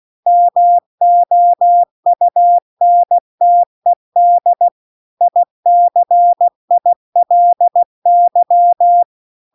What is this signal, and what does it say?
Morse code, heard as a keyed tone, about 16 wpm, 700 Hz. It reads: MOUNTED ICILY